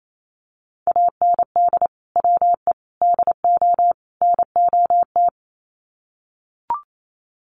Morse code, heard as a keyed tone, about 28 wpm, 700 Hz. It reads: AND WE DO NOT
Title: Sets of 4 Words 28wpm